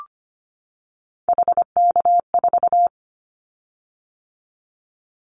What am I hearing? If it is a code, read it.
Morse code, heard as a keyed tone, about 25 wpm, 700 Hz. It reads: HK4
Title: Callsign Prefixes for Top 100 DXCC - 25wpm